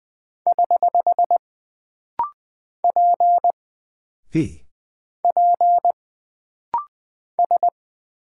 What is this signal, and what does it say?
Morse code, heard as a keyed tone, about 20 wpm, 700 Hz. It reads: <HH> P P S
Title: Single Character (Letter:Number:Punctuation:Prosign) - Rapid-Fire - 20wpm